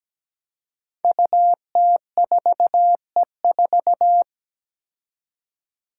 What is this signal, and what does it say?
Morse code, heard as a keyed tone, about 17 wpm, 700 Hz. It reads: UT4E4
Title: Five Challenging Letter-Number Combinations 17wpm